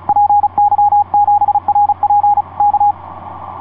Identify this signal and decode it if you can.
Morse code, heard as a keyed tone, about 36 wpm, 810 Hz. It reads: PY7RP K
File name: py7rp dl3jxn etc 12m cw 07mar2010 1614